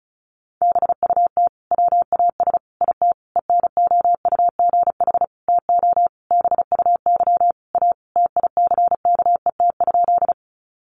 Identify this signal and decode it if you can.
Morse code, heard as a keyed tone, about 34 wpm, 700 Hz. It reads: BUT WAS IT ENOUGH TO BUY A TICKET?
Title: Sentences from Top 1400 Words 35wpm